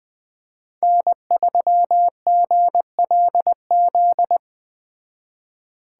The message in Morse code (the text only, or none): N3GLZ